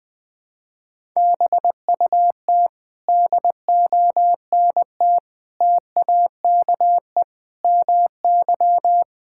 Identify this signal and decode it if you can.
Morse code, heard as a keyed tone, about 20 wpm, 700 Hz. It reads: BUT DONT TAKE MY
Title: Sentences with 7-Word Phrases - 20wpm